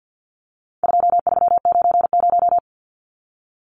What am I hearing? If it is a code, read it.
Morse code, heard as a keyed tone, about 50 wpm, 700 Hz. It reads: 2390
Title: Numbers - 4-Digit 50wpm